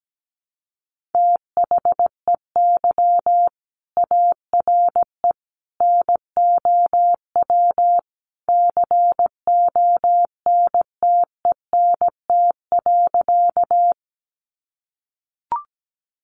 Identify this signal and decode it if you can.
Morse code, heard as a keyed tone, about 17 wpm, 700 Hz. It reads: THEY ARE NOW CONTENT.
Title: Sentences from Top 900 Words 17wpm